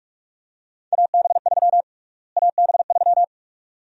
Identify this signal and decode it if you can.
Morse code, heard as a keyed tone, about 43 wpm, 690 Hz. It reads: AB3 AB3
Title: USA Call Sign Prefixes - Repeated 2x - 45wpm